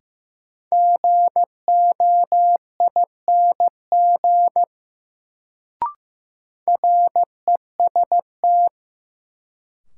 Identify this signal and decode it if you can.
Morse code, heard as a keyed tone, about 15 wpm, 700 Hz. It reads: GOING REST
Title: Sentences from Top 500 Words - New Word Drill - 15wpm